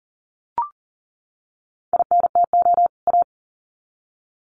Morse code, heard as a keyed tone, about 40 wpm, 700 Hz. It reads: INTO A
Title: Sets of 2 Words 40wpm